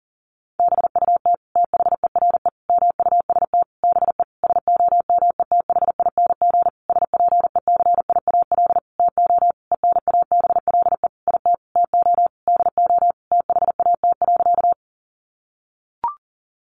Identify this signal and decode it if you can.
Morse code, heard as a keyed tone, about 40 wpm, 700 Hz. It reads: BUT THERE MUST BE SOMETHING SPECIAL TO ENABLE IT TO DO THAT.
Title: Sentences from Top 1500 Words 40wpm